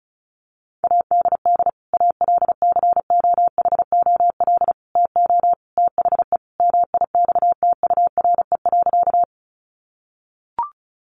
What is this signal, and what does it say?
Morse code, heard as a keyed tone, about 34 wpm, 700 Hz. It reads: ADD ALCOHOL TO THE MIXTURE.
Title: Sentences from Top 1500 Words 35wpm